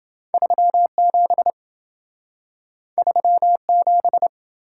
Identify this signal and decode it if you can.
Morse code, heard as a keyed tone, about 27 wpm, 700 Hz. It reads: 37 37